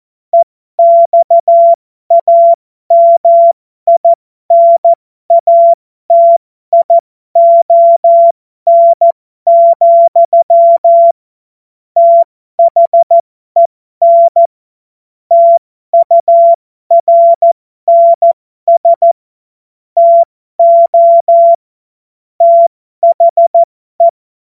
Morse code, heard as a keyed tone, about 14 wpm, spaced as for 12 wpm, 680 Hz. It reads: EXAMINATION, THEN TURNS TO THE